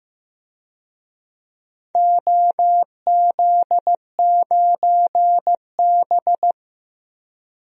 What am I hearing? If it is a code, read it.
Morse code, heard as a keyed tone, about 15 wpm, 700 Hz. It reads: OZ9B
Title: DXCC-100 Call Signs - 4 Characters - Repeated 2x - 15wpm